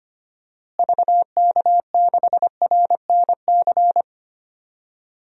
Morse code, heard as a keyed tone, about 25 wpm, 700 Hz. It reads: VK6RNC
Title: DXCC-100 Call Signs - 6 Characters - Repeated 3x - 25wpm